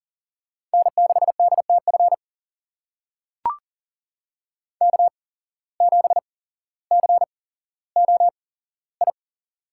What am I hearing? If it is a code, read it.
Morse code, heard as a keyed tone, about 39 wpm, spaced as for 17 wpm, 700 Hz. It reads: N6DTF K7COI
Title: USA Call Signs - 5 Characters - 8x Character Spacing - 40wpm